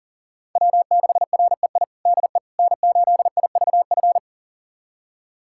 Morse code, heard as a keyed tone, about 40 wpm, 700 Hz. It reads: W6REI DE N8IVF
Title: USA Callsign DE Callsign 40wpm